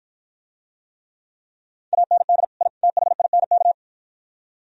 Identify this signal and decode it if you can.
Morse code, heard as a keyed tone, about 52 wpm, 690 Hz. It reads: AND I THINK